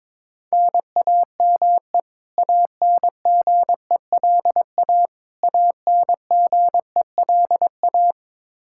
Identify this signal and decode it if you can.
Morse code, heard as a keyed tone, about 22 wpm, 700 Hz. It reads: NAME ANGELA ANGELA
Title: Practice QSO - Exchange 1 - Beginning - 22wpm